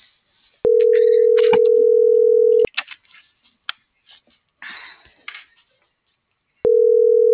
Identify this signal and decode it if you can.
Ringback tone (North America), 440 + 480 Hz.